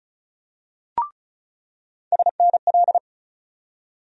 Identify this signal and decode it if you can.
Morse code, heard as a keyed tone, about 36 wpm, 700 Hz. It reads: SNL
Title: USA Call Sign Postfixes - 3 Letters 35wpm